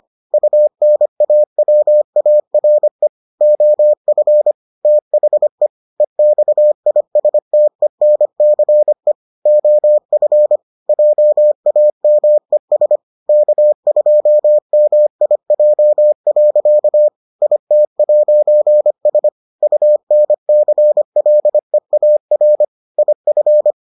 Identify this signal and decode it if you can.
Morse code, heard as a keyed tone, about 25 wpm, 600 Hz. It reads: UNAWARE OF THE EXISTENCE OF JAMES K2MIJ. IT'S UNCLEAR IF